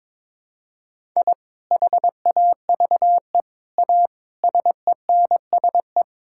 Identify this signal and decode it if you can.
Morse code, heard as a keyed tone, about 22 wpm, 700 Hz. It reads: I HAVE A SENSE
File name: practice / 22wpm / Sentences with 3-Word Phrases - 22wpm